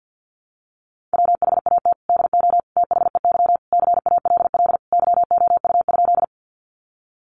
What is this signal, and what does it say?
Morse code, heard as a keyed tone, about 50 wpm, 700 Hz. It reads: WHAT DO THEY CALL YOU?
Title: Sentences from Top 100 Words 50wpm